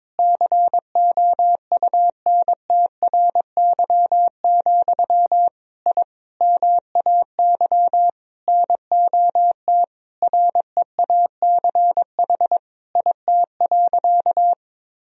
Morse code, heard as a keyed tone, about 22 wpm, 700 Hz. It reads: COUNTRY, I MAY NOT REACH IT.
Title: Sentences from Top 1000 Words - Review Part 2 - 22wpm